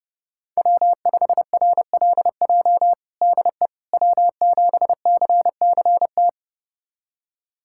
Morse code, heard as a keyed tone, about 30 wpm, 700 Hz. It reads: W5RLJ DE W7CCT
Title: USA Callsign DE Callsign - 5 Characters - 30wpm